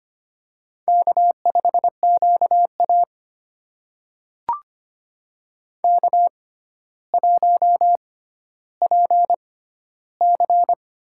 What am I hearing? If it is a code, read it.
Morse code, heard as a keyed tone, about 25 wpm, spaced as for 13 wpm, 700 Hz. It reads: K5QA K1PC
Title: USA Call Signs - 4 Characters - 6x Character Spacing - 25wpm